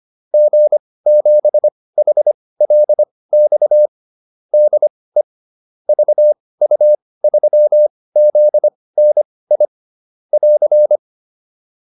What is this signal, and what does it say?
Morse code, heard as a keyed tone, about 25 wpm, spaced as for 18 wpm, 600 Hz. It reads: G7HLX DE VU3ZNI <AR>
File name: QSO410 18